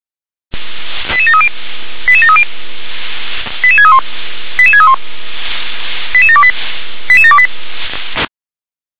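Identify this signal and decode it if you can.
Five-tone selective call (ZVEI), heard as quick runs of five tones, tones of about 70 ms. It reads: 88730 88730 88631 88631 88728 88728
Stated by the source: ZVEI five-tone selective call 88702 received via SDR.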